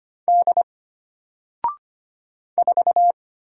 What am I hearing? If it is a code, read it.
Morse code, heard as a keyed tone, about 25 wpm, 700 Hz. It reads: D 4
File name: Single-Letter-Number-25wpm